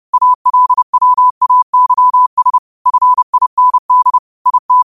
Morse code, heard as a keyed tone, about 30 wpm, 1000 Hz. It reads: ALWAYS FIND IT
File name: qotdfiles / 30wpm